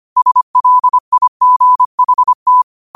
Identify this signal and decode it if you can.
Morse code, heard as a keyed tone, about 25 wpm, 1000 Hz. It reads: SLIGHT
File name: qotdfiles / 25wpm